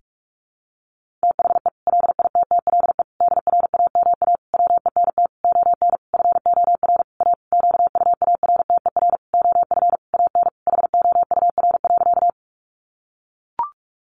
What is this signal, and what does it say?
Morse code, heard as a keyed tone, about 44 wpm, 700 Hz. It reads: THE LITTLE DRAMA WENT ON FOR A QUARTER OF AN HOUR.
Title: Sentences from Top 1500 Words 45wpm